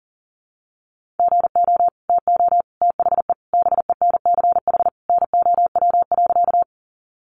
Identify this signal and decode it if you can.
Morse code, heard as a keyed tone, about 40 wpm, 700 Hz. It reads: GO TO THE BENCH NOW.
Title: Sentences from Top 1400 Words 40wpm